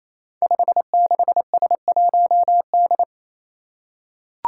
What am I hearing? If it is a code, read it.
Morse code, heard as a keyed tone, about 27 wpm, 700 Hz. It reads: 56S1D
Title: Five Challenging Letter-Number Combinations 28wpm